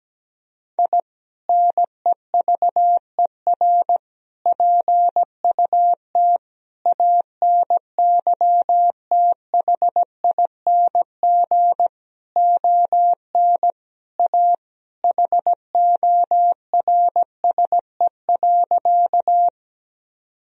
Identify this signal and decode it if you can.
Morse code, heard as a keyed tone, about 17 wpm, 700 Hz. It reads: I NEVER PUT ANYTHING ON A HORSE.
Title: Sentences from Top 1000 Words - Review Part 1 - 17wpm